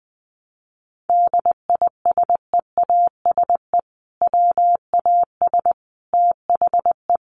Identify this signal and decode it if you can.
Morse code, heard as a keyed tone, about 20 wpm, 700 Hz. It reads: DISEASE WAS THE
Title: Sentences from Top 1100 Words 20wpm